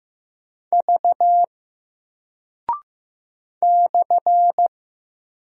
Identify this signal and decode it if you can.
Morse code, heard as a keyed tone, about 15 wpm, 700 Hz. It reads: V /
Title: Single Character (Letter:Number:Punctuation:Prosign) 15wpm